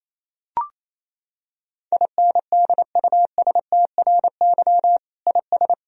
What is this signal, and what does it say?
Morse code, heard as a keyed tone, about 27 wpm, 700 Hz. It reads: INDUSTRY IS
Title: Sentences from Top 1000 Words - Review Part 4 - 28wpm